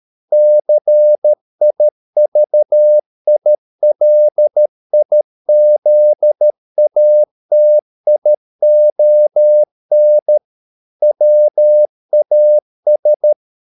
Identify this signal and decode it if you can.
Morse code, heard as a keyed tone, about 13 wpm, 600 Hz. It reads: CIVILIZATION WAS